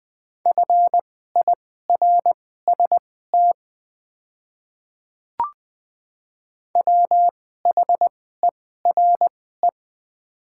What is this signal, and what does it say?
Morse code, heard as a keyed tone, about 20 wpm, spaced as for 14 wpm, 700 Hz. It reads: FIRST WHERE
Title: 2x Character Spacing - Top 200 Words - 20wpm